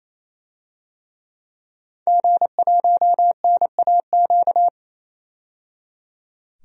Morse code, heard as a keyed tone, about 28 wpm, 700 Hz. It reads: G1NAQ
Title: DXCC-100 Call Signs - 5 Characters - 28wpm